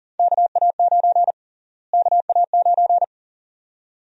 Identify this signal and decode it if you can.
Morse code, heard as a keyed tone, about 40 wpm, 700 Hz. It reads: KA9 KA9
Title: USA Call Sign Prefixes - Repeated 2x - 40wpm